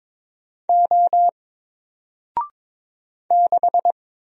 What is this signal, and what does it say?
Morse code, heard as a keyed tone, about 22 wpm, 700 Hz. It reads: O 6
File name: Single-Letter-Number-22wpm